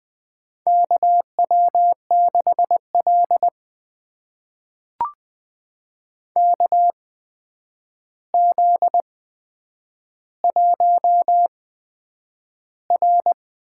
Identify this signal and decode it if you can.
Morse code, heard as a keyed tone, about 20 wpm, spaced as for 9 wpm, 700 Hz. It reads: KW6L KZ1R
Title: USA Call Signs - 4 Characters - 8x Character Spacing - 20wpm